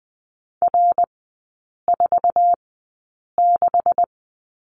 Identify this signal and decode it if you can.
Morse code, heard as a keyed tone, about 20 wpm, spaced as for 8 wpm, 700 Hz. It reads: R46
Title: 184 Lesson 30e - Letter b - Three Characters 20wpm